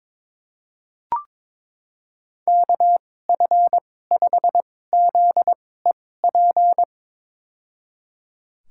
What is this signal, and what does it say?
Morse code, heard as a keyed tone, about 22 wpm, spaced as for 16 wpm, 700 Hz. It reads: KF5ZEP
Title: USA Call Signs - 6 Characters - 2x Character Spacing - 22wpm